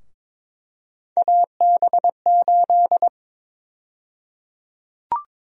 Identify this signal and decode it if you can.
Morse code, heard as a keyed tone, about 22 wpm, 700 Hz. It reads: AB8
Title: USA Call Sign Prefixes - Repeated 2x - 22wpm